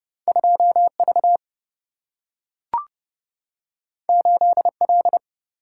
Morse code, heard as a keyed tone, about 31 wpm, 700 Hz. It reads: 2 V 8 L
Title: Single-Letter Single-Number 30wpm